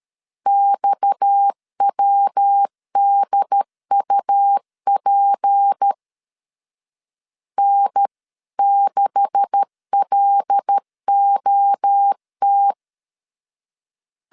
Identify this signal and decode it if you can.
Morse code, heard as a keyed tone, about 13 wpm, 800 Hz. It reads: XWDUP N6LOT
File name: LEZIONI / 10 SIGNS / 10D COMA 12wpm 2544